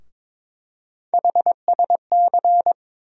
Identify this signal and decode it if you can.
Morse code, heard as a keyed tone, about 22 wpm, 700 Hz. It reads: HSC
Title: USA Call Sign Postfixes - 3 Letters 22wpm